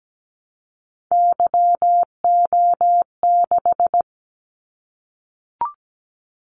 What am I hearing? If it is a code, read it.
Morse code, heard as a keyed tone, about 17 wpm, 700 Hz. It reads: YO6
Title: Callsign Prefixes for Top 25 DXCC - 17wpm